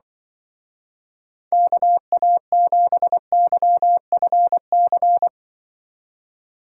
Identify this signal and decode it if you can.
Morse code, heard as a keyed tone, about 24 wpm, 700 Hz. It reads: KA7YFC
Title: Speed Racing - USA 2x3 Call Signs 20wpm